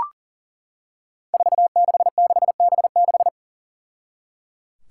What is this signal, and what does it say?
Morse code, heard as a keyed tone, about 37 wpm, 700 Hz. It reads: 466B6
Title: Five Challenging Letter-Number Combinations 40wpm